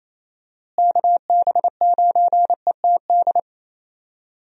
Morse code, heard as a keyed tone, about 28 wpm, 700 Hz. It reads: KB9ETD
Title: USA Call Signs - 6 Characters - Repeated 3x - 28wpm